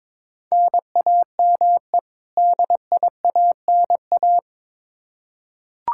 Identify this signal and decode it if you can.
Morse code, heard as a keyed tone, about 21 wpm, 700 Hz. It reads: NAME DIANA
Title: Practice QSO - Exchange 1 - Beginning - 22wpm